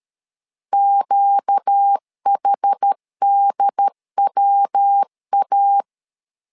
Morse code, heard as a keyed tone, about 13 wpm, 800 Hz. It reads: QHDWA
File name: LEZIONI / 4 Q1 12wpm 2028